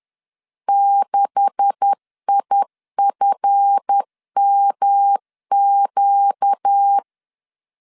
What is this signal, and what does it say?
Morse code, heard as a keyed tone, about 10 wpm, 800 Hz. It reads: 6IFMQ
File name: LEZIONI / 9 2867 10wpm 2733